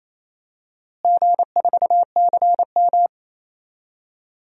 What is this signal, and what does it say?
Morse code, heard as a keyed tone, about 28 wpm, 700 Hz. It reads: G4CM
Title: DXCC-100 Call Signs - 4 Characters - Repeated 3x - 28wpm